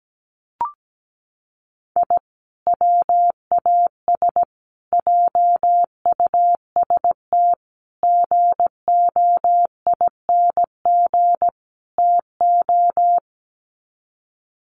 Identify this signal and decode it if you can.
Morse code, heard as a keyed tone, about 17 wpm, 700 Hz. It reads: I WAS JUST GOING TO
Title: Sets of 5 Words Encore 17wpm